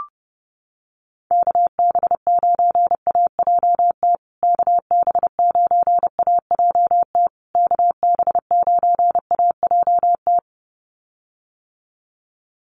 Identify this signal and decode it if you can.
Morse code, heard as a keyed tone, about 28 wpm, 700 Hz. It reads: KB9AJT KB9AJT KB9AJT
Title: USA Call Signs Repeated 3x 30wpm